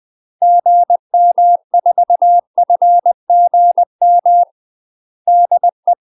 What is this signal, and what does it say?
Morse code, heard as a keyed tone, about 20 wpm, 700 Hz. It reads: GM4FGM DE
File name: UK MASTER QSOs TRAINING / 20WPM / UK MASTER QSOs 0073